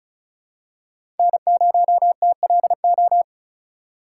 Morse code, heard as a keyed tone, about 36 wpm, 700 Hz. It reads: N0TLO
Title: USA Call Signs - 5 Characters - 4x Character Spacing - 35wpm